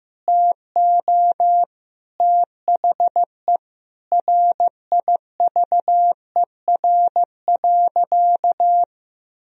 Morse code, heard as a keyed tone, about 15 wpm, 700 Hz. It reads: TO THE RIVER.
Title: Sentences from Top 1000 Words - Review Part 1 - 15wpm